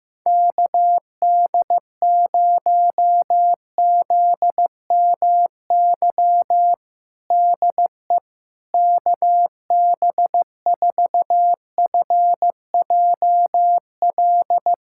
Morse code, heard as a keyed tone, about 15 wpm, 700 Hz. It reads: KD0ZMY DE KB4FJL
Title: USA Callsign DE Callsign - 6 Characters - 15wpm